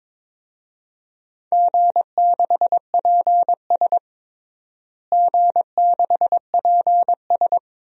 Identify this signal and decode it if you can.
Morse code, heard as a keyed tone, about 21 wpm, 700 Hz. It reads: G6PS G6PS
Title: DXCC-100 Call Signs - 4 Characters - Repeated 2x - 22wpm